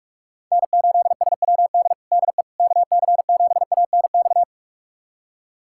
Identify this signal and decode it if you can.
Morse code, heard as a keyed tone, about 41 wpm, 690 Hz. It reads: N8SWD DE KC7ANX
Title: USA Callsign DE Callsign 45wpm